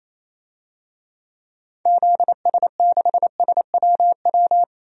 Morse code, heard as a keyed tone, about 28 wpm, 700 Hz. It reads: ZS6SWW
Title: DXCC-100 Call Signs - 6 Characters - 28wpm